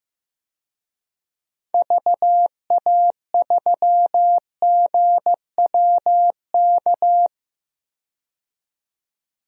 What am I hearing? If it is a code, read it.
Morse code, heard as a keyed tone, about 15 wpm, 700 Hz. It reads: VA3GWK
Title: DXCC-100 Call Signs - 6 Characters - 15wpm